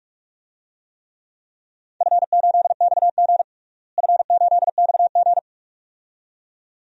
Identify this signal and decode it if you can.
Morse code, heard as a keyed tone, about 46 wpm, 700 Hz. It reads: F8XG F8XG
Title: DXCC-100 Call Signs - 4 Characters - Repeated 2x - 45wpm